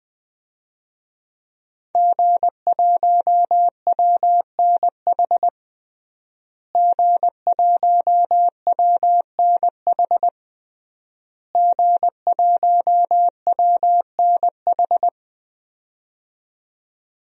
Morse code, heard as a keyed tone, about 20 wpm, 700 Hz. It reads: G1WNH G1WNH G1WNH
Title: DXCC-100 Call Signs - 5 Characters - Repeated 3x - 20wpm